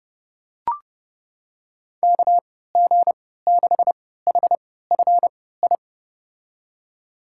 Morse code, heard as a keyed tone, about 32 wpm, spaced as for 17 wpm, 700 Hz. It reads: KG6HFI
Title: USA Call Signs - 6 Characters - 3x Character Spacing - 30wpm